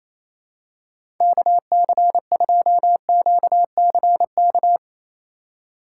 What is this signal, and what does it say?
Morse code, heard as a keyed tone, about 28 wpm, 700 Hz. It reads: KC2QCK
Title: USA Call Signs - 6 Characters - 2x Character Spacing - 28wpm